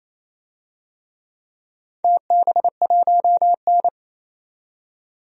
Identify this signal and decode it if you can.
Morse code, heard as a keyed tone, about 28 wpm, 700 Hz. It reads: TB1N
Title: DXCC-100 Call Signs - 4 Characters - 28wpm